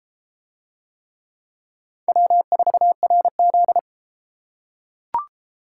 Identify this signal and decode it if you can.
Morse code, heard as a keyed tone, about 31 wpm, 700 Hz. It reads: W4RZ